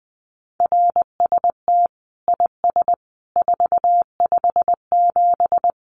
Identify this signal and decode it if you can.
Morse code, heard as a keyed tone, about 20 wpm, 700 Hz. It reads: RST IS 457
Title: Common QSO Phrases 20wpm